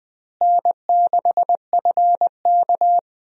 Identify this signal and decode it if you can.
Morse code, heard as a keyed tone, about 20 wpm, 700 Hz. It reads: N6FK
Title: USA Call Signs - 4 Characters - Repeated 3x - 20wpm